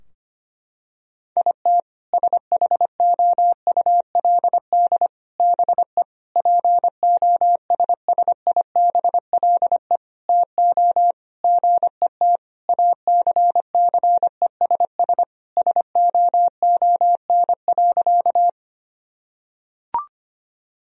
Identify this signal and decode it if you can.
Morse code, heard as a keyed tone, about 25 wpm, 700 Hz. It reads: IT SHOULD BE POSSIBLE TO GET ACCESS SOON.
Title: Sentences from Top 1000 Words - Review Part 4 - 25wpm